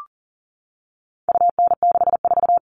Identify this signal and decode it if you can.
Morse code, heard as a keyed tone, about 40 wpm, 700 Hz. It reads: UN64
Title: All Cut Numbers - 4-Digit 40wpm